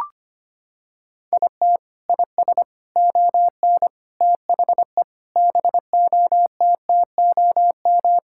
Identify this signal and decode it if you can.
Morse code, heard as a keyed tone, about 25 wpm, 700 Hz. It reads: IT IS ON THE BOTTOM
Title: Sentences from Top 1000 Words - Review Part 6 - 25wpm